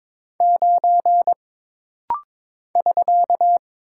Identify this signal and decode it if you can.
Morse code, heard as a keyed tone, about 22 wpm, 700 Hz. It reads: 9 <SK>
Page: This rapid-fire timing is geared to push your ICR (Instant Character Recognition) skills.